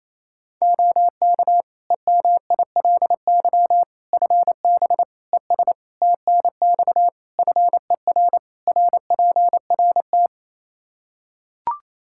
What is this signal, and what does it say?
Morse code, heard as a keyed tone, about 27 wpm, 700 Hz. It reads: OK EMILY FB ES TNX FER RPRT
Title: Practice QSO - Exchange 2 - Info - 28wpm